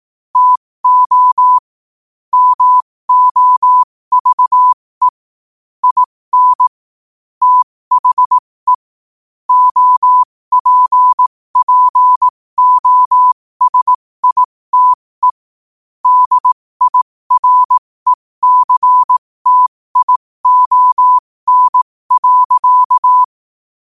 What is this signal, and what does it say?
Morse code, heard as a keyed tone, about 18 wpm, spaced as for 15 wpm, 1000 Hz. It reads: TO MOVE IN THE OPPOSITE DIRECTION.
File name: qotdfiles / 15wpm